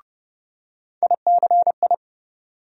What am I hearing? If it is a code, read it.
Morse code, heard as a keyed tone, about 30 wpm, 700 Hz. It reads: ICI
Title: USA Call Sign Postfixes - 3 Letters 30wpm